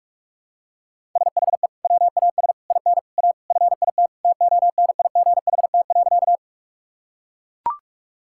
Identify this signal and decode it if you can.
Morse code, heard as a keyed tone, about 42 wpm, 700 Hz. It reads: SHE WAS IN A FIT TONIGHT.
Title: Sentences from Top 1000 Words - Review Part 5 - 45wpm